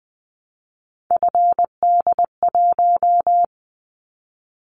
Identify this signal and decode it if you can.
Morse code, heard as a keyed tone, about 20 wpm, 700 Hz. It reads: FD1
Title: Callsign Prefixes for Top 25 DXCC - 20wpm